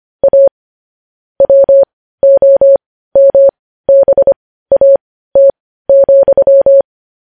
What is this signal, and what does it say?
Morse code, heard as a keyed tone, about 25 wpm, spaced as for 15 wpm, 550 Hz. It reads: A WOMBAT,
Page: news headlines in morse code at 15 wpm